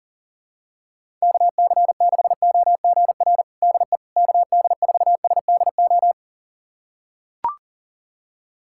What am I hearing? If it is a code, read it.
Morse code, heard as a keyed tone, about 39 wpm, 700 Hz. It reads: KC6OGR DE KD4SDO
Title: USA Callsign DE Callsign 40wpm